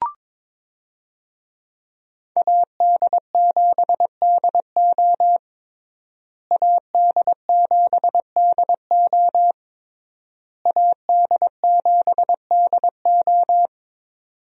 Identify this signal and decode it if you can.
Morse code, heard as a keyed tone, about 22 wpm, 700 Hz. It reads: AD7DO AD7DO AD7DO
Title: USA Call Signs - 5 Characters - Repeated 3x - 22wpm